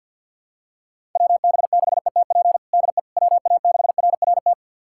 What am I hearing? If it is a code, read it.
Morse code, heard as a keyed tone, about 51 wpm, 700 Hz. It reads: WB6ETP DE WA6RRT
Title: USA Callsign DE Callsign - 6 Characters - 50wpm